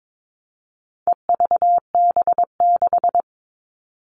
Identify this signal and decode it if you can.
Morse code, heard as a keyed tone, about 22 wpm, 700 Hz. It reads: EVB6
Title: All Cut Numbers - 4-Digit 22wpm